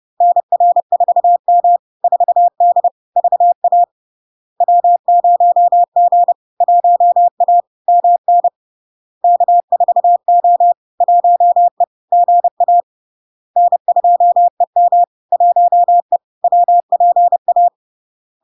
Morse code, heard as a keyed tone, about 30 wpm, 700 Hz. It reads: NR4M 4D VA W0G 1A MN K4O 1E GA N2EM 1E WPA
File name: LARK-FD-2022-Log-in-CW-30wpm-700-Hz